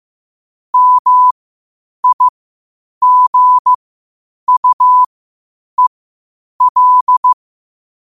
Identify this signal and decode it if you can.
Morse code, heard as a keyed tone, about 15 wpm, spaced as for 8 wpm, 1000 Hz. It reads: MIGUEL
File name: qotdfiles / 8wpm